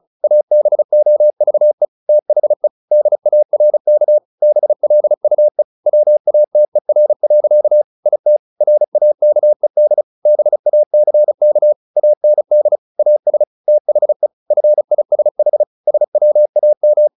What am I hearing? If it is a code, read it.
Morse code, heard as a keyed tone, about 34 wpm, 600 Hz. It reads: ABOVE THE DARK BLUE WATER. IT RAKED BACK AND AS THE FISH SWAM
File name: the old man and the sea 2 35wpm